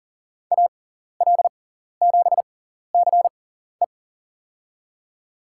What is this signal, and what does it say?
Morse code, heard as a keyed tone, about 40 wpm, spaced as for 14 wpm, 700 Hz. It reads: AL7CE
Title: USA Call Signs - 5 Characters - 6x Character Spacing - 40wpm